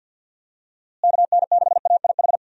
Morse code, heard as a keyed tone, about 51 wpm, 700 Hz. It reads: KN6AIH
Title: USA Call Signs - 6 Characters - Repeated 2x - 50wpm